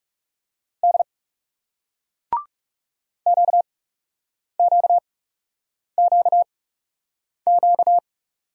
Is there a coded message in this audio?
Yes: Morse code from 0.8 to 8.0 s, heard as a keyed tone, about 39 wpm, spaced as for 8 wpm, 700 Hz. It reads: D QQQQ